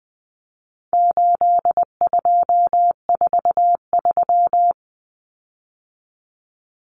Morse code, heard as a keyed tone, about 20 wpm, 700 Hz. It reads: 8243
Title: Common Cut Numbers - 4-Digit 20wpm